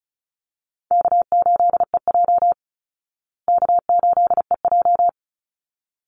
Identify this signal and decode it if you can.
Morse code, heard as a keyed tone, about 34 wpm, 700 Hz. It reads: K8EJ K8EJ
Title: Call Signs Repeated 3x - 35wpm